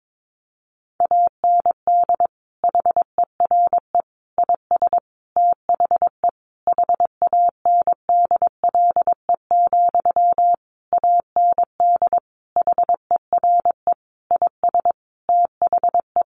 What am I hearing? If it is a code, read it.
Morse code, heard as a keyed tone, about 22 wpm, 700 Hz. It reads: AND HERE IS THE HANDLE, AND HERE IS THE
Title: Sentences from Top 1100 Words 22wpm